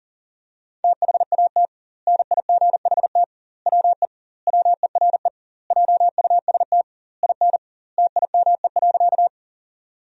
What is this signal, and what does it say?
Morse code, heard as a keyed tone, about 40 wpm, 700 Hz. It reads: THAT NIGHT WE WERE JUST IN TIME.
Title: 2x Word-Spacing - Sentences from Top 300 Words - 40wpm